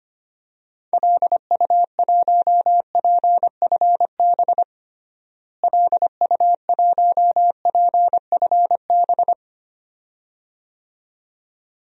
Morse code, heard as a keyed tone, about 25 wpm, 700 Hz. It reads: LU1PFB LU1PFB
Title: DXCC-100 Call Signs - 6 Characters - Repeated 2x - 25wpm